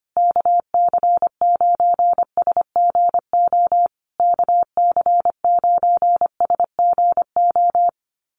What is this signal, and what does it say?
Morse code, heard as a keyed tone, about 25 wpm, 700 Hz. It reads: KC9SGO KC9SGO
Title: USA Call Signs Repeated 3x 25wpm